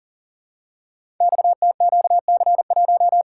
Morse code, heard as a keyed tone, about 42 wpm, 700 Hz. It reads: XTQC1
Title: Code Groups - 5 Letter:Numbers - 40wpm